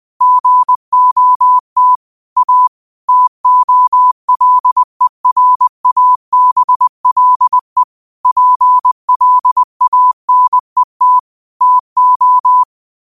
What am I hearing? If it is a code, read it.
Morse code, heard as a keyed tone, about 20 wpm, 1000 Hz. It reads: GOT A TOLERABLE PLANET TO
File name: qotdfiles / 20wpm